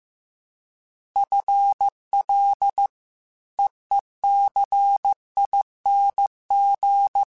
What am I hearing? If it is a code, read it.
Morse code, heard as a keyed tone, about 15 wpm, 800 Hz. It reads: FL EECING
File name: FL / FL mp3s / FL Prefixes 3x2 sp @13